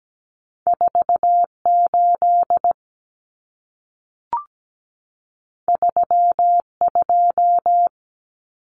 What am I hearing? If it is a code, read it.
Morse code, heard as a keyed tone, about 17 wpm, 700 Hz. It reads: 48 32
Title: Common Cut Numbers - 2-Digit 17wpm